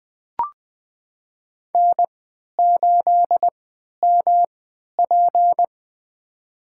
Morse code, heard as a keyed tone, about 21 wpm, spaced as for 11 wpm, 700 Hz. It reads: N8MP
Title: USA Call Signs - 4 Characters - 3x Character Spacing - 20wpm